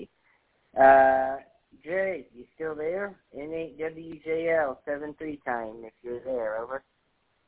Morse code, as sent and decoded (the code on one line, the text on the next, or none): none